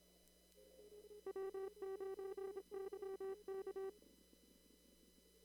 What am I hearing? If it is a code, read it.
Morse code, heard as a keyed tone, about 26 wpm, 770 Hz, with noise. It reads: W9YK
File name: CW / kt5j wpx cw-op-k5pi